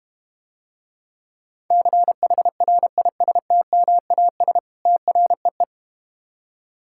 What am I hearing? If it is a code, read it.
Morse code, heard as a keyed tone, about 30 wpm, 700 Hz. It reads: CHRISTMAS TREE
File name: Easy Words - Speed-Racing with Hints - 25wpm